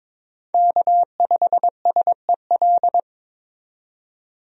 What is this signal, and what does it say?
Morse code, heard as a keyed tone, about 22 wpm, 700 Hz. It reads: K5SEL
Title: USA Call Signs - 5 Characters - 22wpm